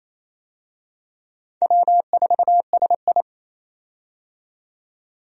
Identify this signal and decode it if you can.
Morse code, heard as a keyed tone, about 28 wpm, 700 Hz. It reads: W4SI
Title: USA Call Signs - 4 Characters - 28wpm